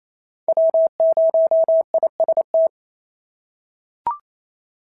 Morse code, heard as a keyed tone, about 28 wpm, 650 Hz. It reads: W0IST
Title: SOTA QSO Practice 28wpm